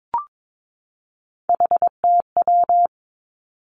Morse code, heard as a keyed tone, about 22 wpm, 700 Hz. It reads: HTW
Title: Three Challenging Letter-Number Combinations 22wpm